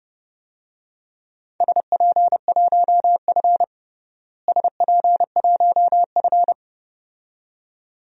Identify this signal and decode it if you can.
Morse code, heard as a keyed tone, about 31 wpm, 700 Hz. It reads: SP1F SP1F
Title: DXCC-100 Call Signs - 4 Characters - Repeated 2x - 30wpm